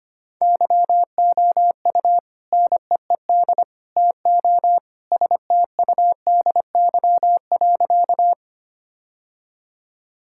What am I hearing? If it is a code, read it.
Morse code, heard as a keyed tone, about 25 wpm, 700 Hz. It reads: YOU NEED TO STUDY.
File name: practice / 25wpm / Sentences with 3-Word Phrases - 25wpm